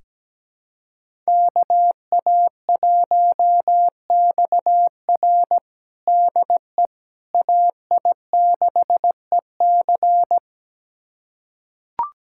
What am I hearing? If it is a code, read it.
Morse code, heard as a keyed tone, about 17 wpm, 700 Hz. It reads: KA1XR DE AI6EC
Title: USA Callsign DE Callsign - 5 Characters - 17wpm